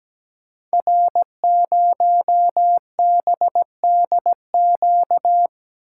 Morse code, heard as a keyed tone, about 17 wpm, 700 Hz. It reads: R0BDQ
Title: Code Groups - 5 Letter:Numbers - 17wpm